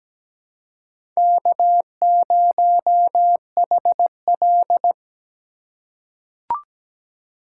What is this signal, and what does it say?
Morse code, heard as a keyed tone, about 17 wpm, 700 Hz. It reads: K0HL
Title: USA Call Signs - 4 Characters - 3x Character Spacing - 17wpm